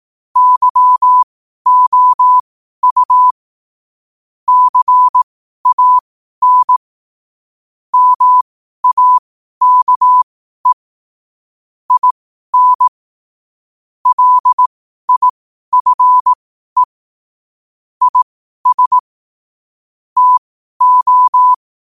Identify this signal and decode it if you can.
Morse code, heard as a keyed tone, about 18 wpm, spaced as for 12 wpm, 1000 Hz. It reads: YOU CAN MAKE IN LIFE IS TO
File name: qotdfiles / 12wpm